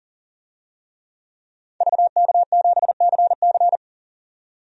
Morse code, heard as a keyed tone, about 38 wpm, 700 Hz. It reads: VK7CC
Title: DXCC-100 Call Signs - 5 Characters - 40wpm